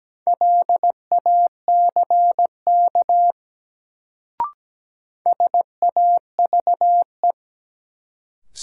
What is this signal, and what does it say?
Morse code, heard as a keyed tone, about 17 wpm, 700 Hz. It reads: LACK SAVE
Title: Sentences from Top 1100 Words - New Word Drill - 17wpm